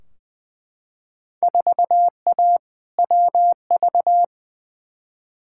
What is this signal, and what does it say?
Morse code, heard as a keyed tone, about 20 wpm, 700 Hz. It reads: 4A WV
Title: Field Day Exchanges Encore - 20wpm